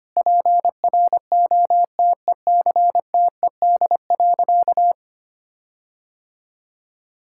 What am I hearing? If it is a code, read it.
Morse code, heard as a keyed tone, about 25 wpm, 700 Hz. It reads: PROTECTED.
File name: practice / 25wpm / Sentences with 9-Word Phrases - 25wpm